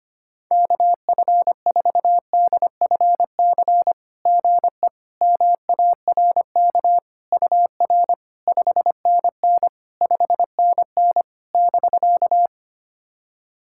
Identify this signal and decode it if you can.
Morse code, heard as a keyed tone, about 25 wpm, 700 Hz. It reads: KF4DFC GE MARK UR 5NN 5NN <BK>
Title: SOTA QSO Practice 25wpm